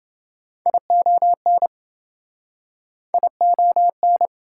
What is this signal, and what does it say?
Morse code, heard as a keyed tone, about 29 wpm, 700 Hz. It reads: ION ION